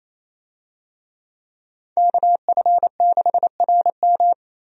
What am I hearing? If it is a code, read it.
Morse code, heard as a keyed tone, about 28 wpm, 700 Hz. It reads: KF6RM
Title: USA Call Signs - 5 Characters - Repeated 3x - 28wpm